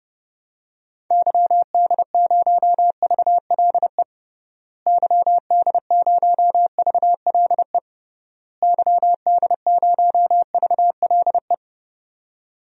Morse code, heard as a keyed tone, about 30 wpm, 700 Hz. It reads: YD0VLE YD0VLE YD0VLE
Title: DXCC-100 Call Signs - 6 Characters - Repeated 3x - 30wpm